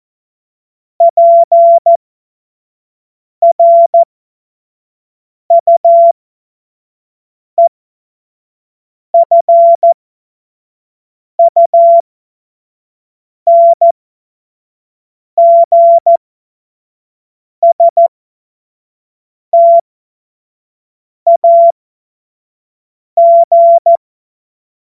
Morse code, heard as a keyed tone, about 14 wpm, spaced as for 5 wpm, 680 Hz. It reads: PRUEFUNGSTAG
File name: Morsekurs / Witze / Witz 17. Mai 2022 25 F70